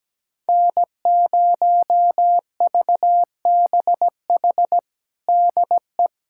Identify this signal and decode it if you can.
Morse code, heard as a keyed tone, about 17 wpm, 700 Hz. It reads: N0VBH DE
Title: USA Callsign DE Callsign 17wpm